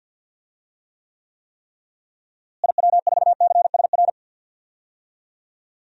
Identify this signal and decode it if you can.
Morse code, heard as a keyed tone, about 51 wpm, 700 Hz. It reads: IW4CSR
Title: DXCC-100 Call Signs - 6 Characters - Repeated 3x - 50wpm